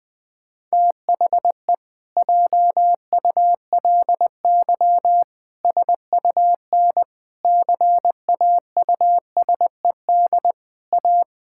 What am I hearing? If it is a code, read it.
Morse code, heard as a keyed tone, about 20 wpm, 700 Hz. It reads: THE JULY SUN CAUSED A
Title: Pangram Challenge v2 - 20wpm